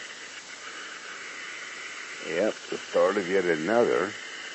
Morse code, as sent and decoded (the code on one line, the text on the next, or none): none